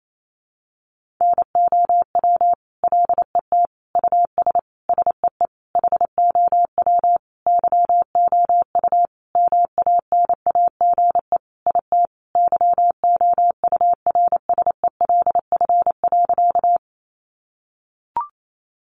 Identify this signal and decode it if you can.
Morse code, heard as a keyed tone, about 28 wpm, 700 Hz. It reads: NOW LET US SEE HOW YOU MANAGE IT YOURSELF.
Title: Sentences from Top 1400 Words 28wpm